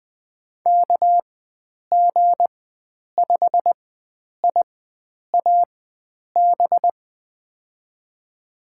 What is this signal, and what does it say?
Morse code, heard as a keyed tone, about 20 wpm, spaced as for 9 wpm, 700 Hz. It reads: KG5IAB
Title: USA Call Signs - 6 Characters - 4x Character Spacing - 20wpm